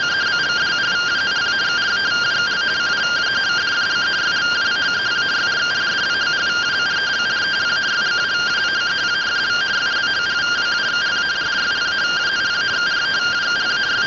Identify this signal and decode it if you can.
Two-tone teleprinter signal, about 170 Hz shift.